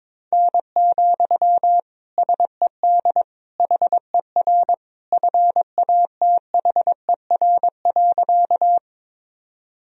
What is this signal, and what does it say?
Morse code, heard as a keyed tone, about 22 wpm, 700 Hz. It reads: N, SED HER FATHER.
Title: Sentences from Top 300 Words and Abbreviations 22wpm